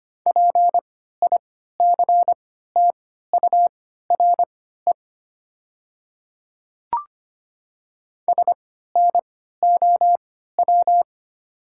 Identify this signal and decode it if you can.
Morse code, heard as a keyed tone, about 25 wpm, spaced as for 14 wpm, 700 Hz. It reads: PICTURE SNOW
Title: 3x Character Spacing - Top 500 Words - 25wpm